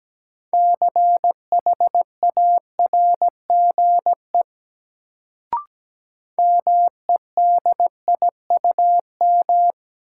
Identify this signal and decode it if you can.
Morse code, heard as a keyed tone, about 17 wpm, 700 Hz. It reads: CHARGE MEDIUM
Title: Sentences from Top 900 Words - New Word Drill - 17wpm